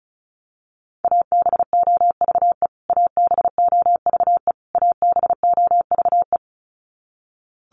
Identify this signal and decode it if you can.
Morse code, heard as a keyed tone, about 34 wpm, 700 Hz. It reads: ABOVE ABOVE ABOVE
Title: Top 500 Words Repeated 35wpm